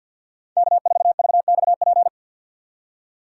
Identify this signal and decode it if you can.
Morse code, heard as a keyed tone, about 50 wpm, 700 Hz. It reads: K4VXP
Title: USA Call Signs - 5 Characters - 3x Character Spacing - 50wpm